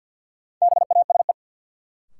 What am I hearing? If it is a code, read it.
Morse code, heard as a keyed tone, about 47 wpm, 700 Hz. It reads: BASE
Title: Sentences from Top 1000 Words - Review Part 1 - New Word Drill - 50wpm